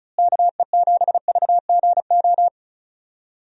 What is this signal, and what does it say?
Morse code, heard as a keyed tone, about 34 wpm, 700 Hz. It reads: KE7VGO
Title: USA Call Signs - 6 Characters - Repeated 3x - 35wpm